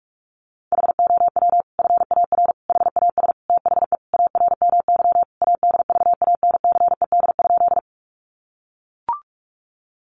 Morse code, heard as a keyed tone, about 44 wpm, 700 Hz. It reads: HOW FAR HAS THE ARMY ADVANCED?
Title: Sentences from Top 1200 Words 45wpm